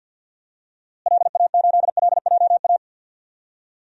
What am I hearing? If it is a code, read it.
Morse code, heard as a keyed tone, about 46 wpm, 700 Hz. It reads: LA8LJA
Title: DXCC-100 Call Signs - 6 Characters - Repeated 2x - 50wpm